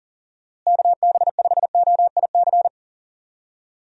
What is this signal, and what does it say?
Morse code, heard as a keyed tone, about 40 wpm, 700 Hz. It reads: KB5OIC
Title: USA Call Signs - 6 Characters - Repeated 2x - 40wpm